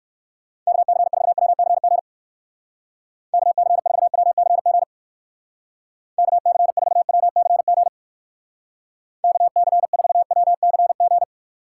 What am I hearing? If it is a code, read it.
Morse code, heard as a keyed tone, about 68 wpm, 700 Hz. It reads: KC4WCG KC4WCG KC4WCG KC4WCG